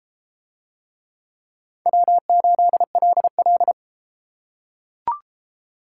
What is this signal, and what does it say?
Morse code, heard as a keyed tone, about 33 wpm, 700 Hz. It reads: W8LL